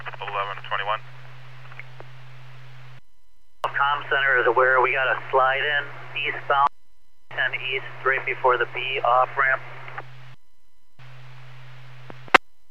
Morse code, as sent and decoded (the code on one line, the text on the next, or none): none